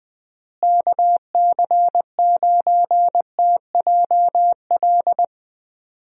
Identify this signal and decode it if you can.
Morse code, heard as a keyed tone, about 20 wpm, 700 Hz. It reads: KC9TJL
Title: USA Call Signs - 6 Characters - Repeated 3x - 20wpm